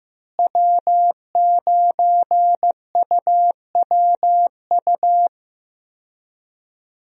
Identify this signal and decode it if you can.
Morse code, heard as a keyed tone, about 15 wpm, 700 Hz. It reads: W9UWU
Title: USA Call Signs - 5 Characters - Repeated 2x - 15wpm